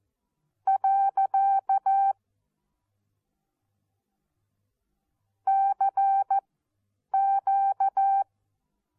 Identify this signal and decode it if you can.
Morse code, heard as a keyed tone, about 14 wpm, spaced as for 8 wpm, 780 Hz. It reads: . CQ
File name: Onlinelehrgang / morsen / DJ4UF Morsen14